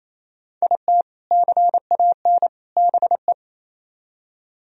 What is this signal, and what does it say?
Morse code, heard as a keyed tone, about 35 wpm, 700 Hz. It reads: IT CAN BE
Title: Speed-Racing - Sets of 3 Words 28wpm